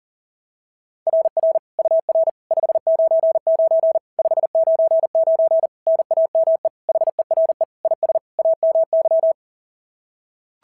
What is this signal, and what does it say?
Morse code, heard as a keyed tone, about 37 wpm, 660 Hz. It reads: RR UR 599 599 NAME HERE IS AMY
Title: SOTA QSO Practice 40wpm